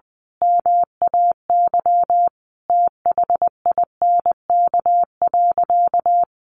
Morse code, heard as a keyed tone, about 20 wpm, 700 Hz. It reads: MAY THINK.
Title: Sentences from Top 1100 Words 20wpm